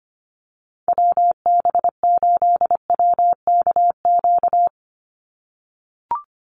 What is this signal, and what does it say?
Morse code, heard as a keyed tone, about 24 wpm, 700 Hz. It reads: WB8WKQ
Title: Call Signs - 25wpm